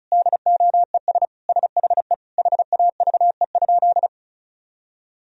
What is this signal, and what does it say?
Morse code, heard as a keyed tone, about 34 wpm, 700 Hz. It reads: DOES SHE HAVE?
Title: Sentences from Top 1000 Words - Review Part 1 - 35wpm